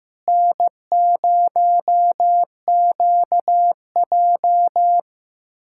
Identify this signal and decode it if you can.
Morse code, heard as a keyed tone, about 15 wpm, 700 Hz. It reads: N0QJ
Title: USA Callsign DE Callsign - 4 Characters - 15wpm